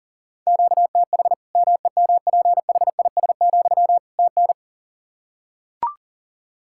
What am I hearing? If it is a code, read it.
Morse code, heard as a keyed tone, about 40 wpm, 700 Hz. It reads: QTH MEMPHIS, TN
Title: Practice QSO - Exchange 1 - Beginning - 40wpm